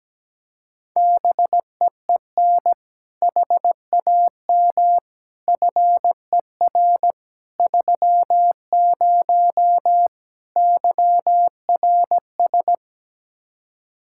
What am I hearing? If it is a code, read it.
Morse code, heard as a keyed tone, about 17 wpm, 700 Hz. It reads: BEEN HAM FER 30 YRS
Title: Speed-Racing Common QSO Phrases - 17wpm